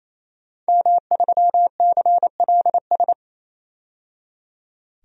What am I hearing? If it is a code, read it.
Morse code, heard as a keyed tone, about 27 wpm, 700 Hz. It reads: M3CLS
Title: DXCC-100 Call Signs - 5 Characters - 28wpm